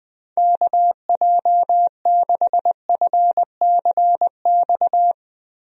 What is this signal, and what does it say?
Morse code, heard as a keyed tone, about 20 wpm, 700 Hz. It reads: KJ6FCX
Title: USA Call Signs - 6 Characters - Repeated 3x - 20wpm